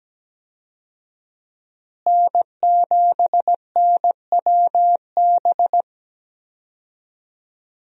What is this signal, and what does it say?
Morse code, heard as a keyed tone, about 17 wpm, 700 Hz. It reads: N7NWB
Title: USA Call Signs - 5 Characters - Repeated 2x - 17wpm